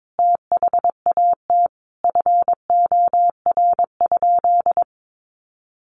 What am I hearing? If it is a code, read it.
Morse code, heard as a keyed tone, about 22 wpm, 700 Hz. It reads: THAT FOR?
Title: Sentences from Top 200 Words 22wpm